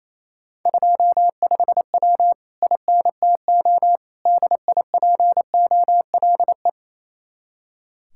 Morse code, heard as a keyed tone, about 34 wpm, 700 Hz. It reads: 25W INTO DIPOLE
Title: Speed-Racing Common QSO Phrases Encore - 28wpm